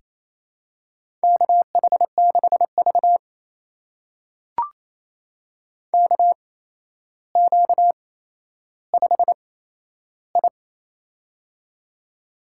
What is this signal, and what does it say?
Morse code, heard as a keyed tone, about 27 wpm, spaced as for 12 wpm, 700 Hz. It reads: KH6V KQ5I
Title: USA Call Signs - 8x Character Spacing - 28wpm